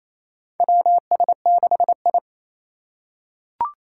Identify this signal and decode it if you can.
Morse code, heard as a keyed tone, about 28 wpm, 700 Hz. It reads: WS6I
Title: USA Call Signs - 4 Characters - 3x Character Spacing - 28wpm